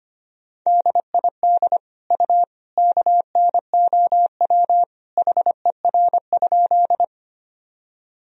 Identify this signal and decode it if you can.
Morse code, heard as a keyed tone, about 25 wpm, 700 Hz. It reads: DID U KNOW HER?
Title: Sentences from Top 100 Words and Abbreviations 25wpm